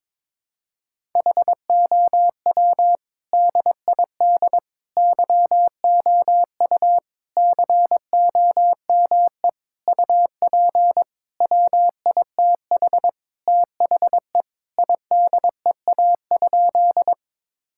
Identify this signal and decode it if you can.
Morse code, heard as a keyed tone, about 22 wpm, 700 Hz. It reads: HOW DID YOU COME UP WITH THE IDEA?
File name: Sentences with 7-Word Phrases - 22wpm